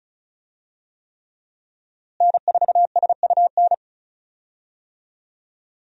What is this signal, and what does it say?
Morse code, heard as a keyed tone, about 36 wpm, 700 Hz. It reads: N4SUN
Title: USA Call Signs - 5 Characters - Repeated 3x - 35wpm